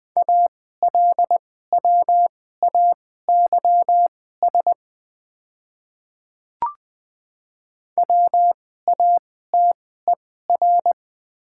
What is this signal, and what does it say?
Morse code, heard as a keyed tone, about 20 wpm, spaced as for 14 wpm, 700 Hz. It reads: ALWAYS WATER
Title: 2x Character Spacing - Top 300 Words - 20wpm